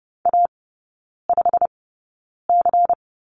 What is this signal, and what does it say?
Morse code, heard as a keyed tone, about 30 wpm, spaced as for 9 wpm, 700 Hz. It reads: A5C
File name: Lesson 16e - Letter c - Three Characters 30wpm